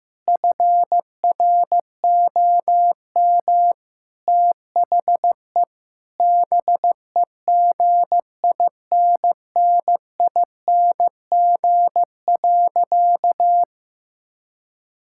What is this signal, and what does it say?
Morse code, heard as a keyed tone, about 15 wpm, 700 Hz. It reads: FROM THE BEGINNING.
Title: Sentences from Top 1000 Words - Review Part 5 - 15wpm